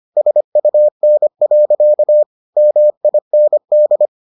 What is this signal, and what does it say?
Morse code, heard as a keyed tone, about 25 wpm, 600 Hz. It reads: SUN. MIND